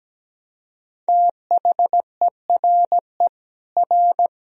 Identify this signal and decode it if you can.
Morse code, heard as a keyed tone, about 17 wpm, 700 Hz. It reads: THERE R
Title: Sentences from Top 200 Words and Abbreviations 17wpm